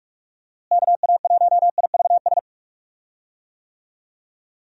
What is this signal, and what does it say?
Morse code, heard as a keyed tone, about 46 wpm, 700 Hz. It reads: KA1IVS
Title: USA Call Signs - 6 Characters - Repeated 2x - 45wpm